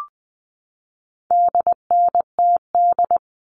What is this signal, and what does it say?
Morse code, heard as a keyed tone, about 20 wpm, 700 Hz. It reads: DNTD
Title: All Cut Numbers - 4-Digit 20wpm